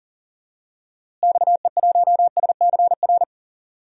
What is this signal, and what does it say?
Morse code, heard as a keyed tone, about 40 wpm, 700 Hz. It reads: XE1SCR
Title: DXCC-100 Call Signs - 6 Characters - Repeated 3x - 40wpm